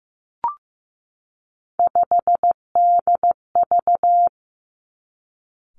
Morse code, heard as a keyed tone, about 15 wpm, 700 Hz. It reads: 5DV
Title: Three Challenging Letter-Number Combinations 15wpm